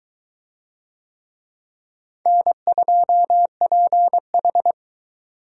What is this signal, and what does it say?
Morse code, heard as a keyed tone, about 20 wpm, 700 Hz. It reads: N2PH